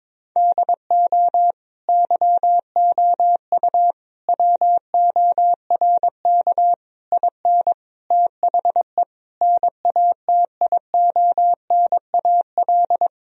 Morse code, heard as a keyed tone, about 22 wpm, 700 Hz. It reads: DO YOU WORK IN THE NATIONAL
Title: Sentences from Top 1000 Words - Review Part 4 - 22wpm